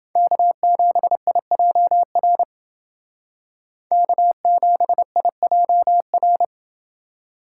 Morse code, heard as a keyed tone, about 28 wpm, 700 Hz. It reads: K7IJR K7IJR